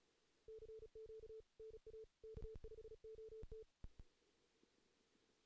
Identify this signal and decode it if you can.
Morse code, heard as a keyed tone, about 36 wpm, 440 Hz. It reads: CQ NA K5OT
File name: 2016 JAN CW NAQP / 2016 Jan naqp k5ot - 0-3